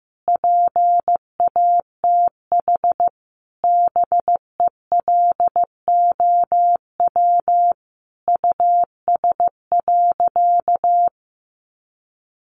Morse code, heard as a keyed tone, about 15 wpm, 700 Hz. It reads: PATH BELOW US.
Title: Sentences from Top 1400 Words 15wpm